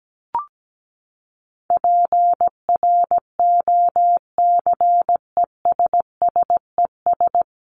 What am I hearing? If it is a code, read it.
Morse code, heard as a keyed tone, about 17 wpm, 700 Hz. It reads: PROCESSES
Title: ES:S Suffix - Top 100 Words with Base Words 17wpm